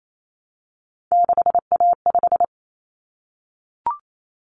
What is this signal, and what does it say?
Morse code, heard as a keyed tone, about 28 wpm, 700 Hz. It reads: 6A5
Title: Common Cut Numbers - 3-Digit 28wpm